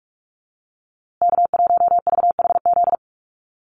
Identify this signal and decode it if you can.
Morse code, heard as a keyed tone, about 44 wpm, 700 Hz. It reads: K1VHZ
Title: USA Call Signs Repeated 2x 45wpm